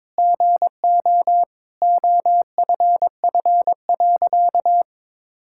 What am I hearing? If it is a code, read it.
Morse code, heard as a keyed tone, about 22 wpm, 700 Hz. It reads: GO OFF.
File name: practice / 22wpm / Sentences with 4-Word Phrases - 22wpm